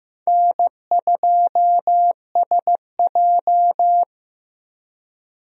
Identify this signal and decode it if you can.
Morse code, heard as a keyed tone, about 15 wpm, 700 Hz. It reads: N2SJ
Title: USA Call Signs - 4 Characters - Repeated 2x - 15wpm